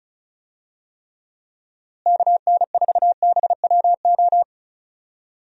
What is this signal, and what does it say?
Morse code, heard as a keyed tone, about 34 wpm, 700 Hz. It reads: KN4BWO
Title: USA Call Signs - 6 Characters - Repeated 2x - 35wpm